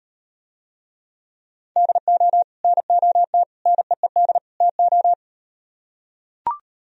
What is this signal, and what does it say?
Morse code, heard as a keyed tone, about 38 wpm, 700 Hz. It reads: DO NOT NEED TO